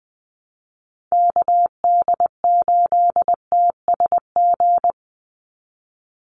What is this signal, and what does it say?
Morse code, heard as a keyed tone, about 20 wpm, 700 Hz. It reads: KD8TSG
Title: USA Call Signs 20wpm